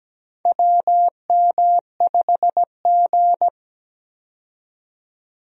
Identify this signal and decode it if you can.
Morse code, heard as a keyed tone, about 17 wpm, 700 Hz. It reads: WM5G
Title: USA Call Signs - 4 Characters - 17wpm